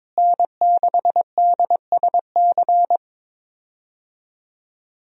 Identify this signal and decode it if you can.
Morse code, heard as a keyed tone, about 21 wpm, 700 Hz. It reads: N6DSC
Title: USA Call Signs - 5 Characters - Repeated 2x - 22wpm